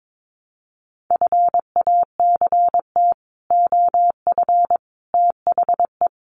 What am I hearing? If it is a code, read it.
Morse code, heard as a keyed tone, about 22 wpm, 700 Hz. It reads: FACT OF THE
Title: Sets of 5 Words 22wpm